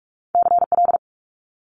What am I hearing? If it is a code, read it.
Morse code, heard as a keyed tone, about 46 wpm, 700 Hz. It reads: CL
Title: Multi-letter Phonemes Quick Fire 45wpm